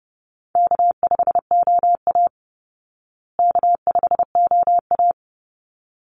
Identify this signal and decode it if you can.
Morse code, heard as a keyed tone, about 30 wpm, 700 Hz. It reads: K5OA K5OA
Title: Easy Call Signs Repeated 2x - 30wpm